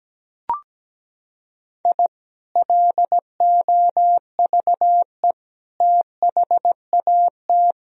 Morse code, heard as a keyed tone, about 17 wpm, 700 Hz. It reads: I LOVE THAT
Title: Sentences from Top 1000 Words - Review Part 4 - 17wpm